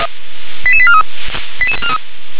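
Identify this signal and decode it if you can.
Five-tone selective call (ZVEI), heard as quick runs of five tones, tones of about 70 ms. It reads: 88843
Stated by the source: ZVEI five-tone selective call to address 88842, received on an SDR.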